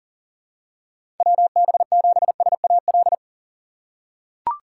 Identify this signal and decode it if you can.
Morse code, heard as a keyed tone, about 37 wpm, 700 Hz. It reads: WB7SAL
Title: USA Call Signs - 6 Characters - 6x Character Spacing - 40wpm